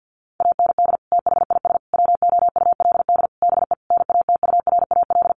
Morse code, heard as a keyed tone, about 50 wpm, 700 Hz. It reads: AND THIS WOULD BE NATURAL
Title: Sentences from Top 1400 Words 50wpm